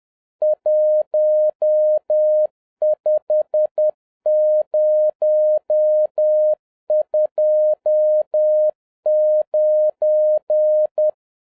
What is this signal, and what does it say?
Morse code, heard as a keyed tone, about 10 wpm, 610 Hz. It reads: 15029